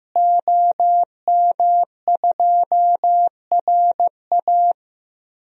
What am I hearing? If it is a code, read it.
Morse code, heard as a keyed tone, about 15 wpm, 700 Hz. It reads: OM2RA
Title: DXCC-100 Call Signs - 5 Characters - Repeated 2x - 15wpm